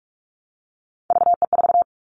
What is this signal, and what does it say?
Morse code, heard as a keyed tone, about 44 wpm, 700 Hz. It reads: VE4
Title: Callsign Prefixes for Top 50 DXCC - 45wpm